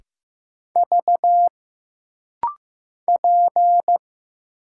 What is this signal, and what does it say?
Morse code, heard as a keyed tone, about 15 wpm, 700 Hz. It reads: V P
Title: Single Letter-Number Alt - Rapid-Fire 15wpm